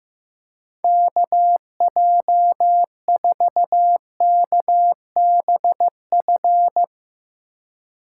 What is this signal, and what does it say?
Morse code, heard as a keyed tone, about 15 wpm, 700 Hz. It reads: KJ4KBF
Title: USA Call Signs - 6 Characters - 3x Character Spacing - 15wpm